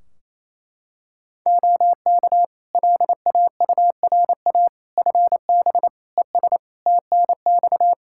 Morse code, heard as a keyed tone, about 28 wpm, 700 Hz. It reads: OK LAURA FB ES TNX
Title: K1USN SST QSO Practice 28wpm